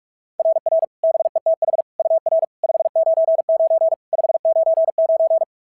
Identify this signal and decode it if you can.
Morse code, heard as a keyed tone, about 44 wpm, 690 Hz. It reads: RR BETH UR 599 599
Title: SOTA QSO Practice 45wpm